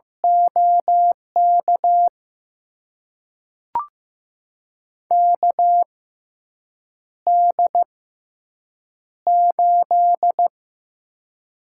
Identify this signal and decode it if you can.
Morse code, heard as a keyed tone, about 15 wpm, spaced as for 7 wpm, 700 Hz. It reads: OK KD8
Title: USA Call Signs - 6 Characters - 6x Character Spacing - 15wpm